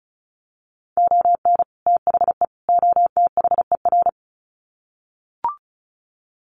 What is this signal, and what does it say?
Morse code, heard as a keyed tone, about 34 wpm, 700 Hz. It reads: ON THE OTHER
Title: Sets of 3 Words 35wpm